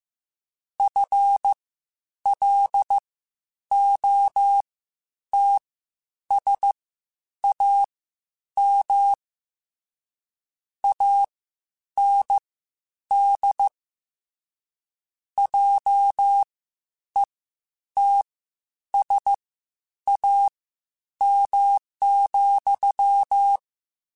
Morse code, heard as a keyed tone, about 15 wpm, spaced as for 9 wpm, 800 Hz. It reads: FLOTSAM AND JETSAM,
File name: Misc / Misc-mp3s / WordPairs sp @13